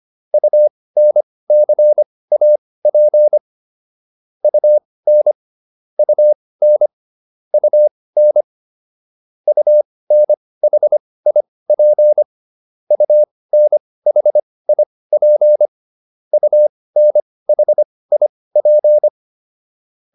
Morse code, heard as a keyed tone, about 25 wpm, spaced as for 18 wpm, 600 Hz. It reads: UNCAP UN UN UN UNHIP UNHIP UNHIP